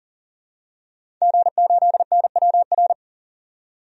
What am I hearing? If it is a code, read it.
Morse code, heard as a keyed tone, about 40 wpm, 700 Hz. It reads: G8NWR
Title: DXCC-100 Call Signs - 5 Characters - 40wpm